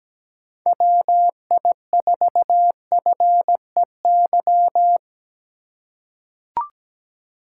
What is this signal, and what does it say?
Morse code, heard as a keyed tone, about 17 wpm, 700 Hz. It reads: WI4FEY
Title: USA Call Signs - 6 Characters - 3x Character Spacing - 17wpm